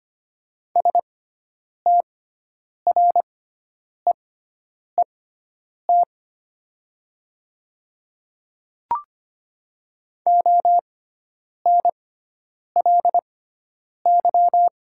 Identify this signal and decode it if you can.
Morse code, heard as a keyed tone, about 25 wpm, spaced as for 9 wpm, 700 Hz. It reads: STREET ONLY
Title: 6x Character Spacing - Top 500 Words - 25wpm